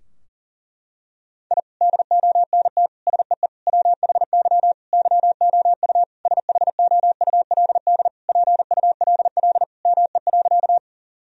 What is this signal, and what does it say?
Morse code, heard as a keyed tone, about 37 wpm, 700 Hz. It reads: I DONT SEE WHY YOU SHOULD PULL ME.
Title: Sentences from Top 1000 Words - Review Part 2 - 40wpm